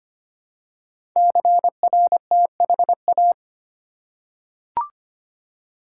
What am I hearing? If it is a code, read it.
Morse code, heard as a keyed tone, about 25 wpm, 700 Hz. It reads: CRTHA
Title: Code Groups - 5 Letters - 25wpm